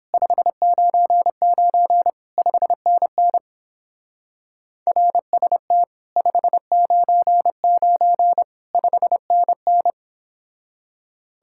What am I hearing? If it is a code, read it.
Morse code, heard as a keyed tone, about 29 wpm, 700 Hz. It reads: 599 5NN RST 599 5NN